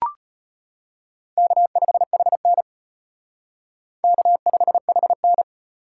Morse code, heard as a keyed tone, about 33 wpm, 700 Hz. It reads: K5HN K5HN